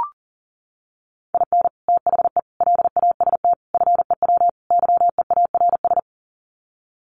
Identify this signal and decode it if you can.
Morse code, heard as a keyed tone, about 40 wpm, 700 Hz. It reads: IN THE LAST FEW YEARS
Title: Sets of 5 Words 40wpm